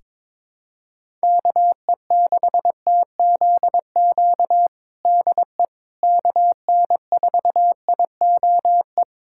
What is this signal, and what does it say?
Morse code, heard as a keyed tone, about 22 wpm, 700 Hz. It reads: KE6TZQ DE KN4IOE
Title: USA Callsign DE Callsign 22wpm